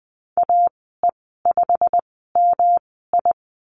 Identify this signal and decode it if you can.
Morse code, heard as a keyed tone, about 20 wpm, spaced as for 14 wpm, 700 Hz. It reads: AE5MI
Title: 186 Lesson 30g - Letter b - Call Signs 20wpm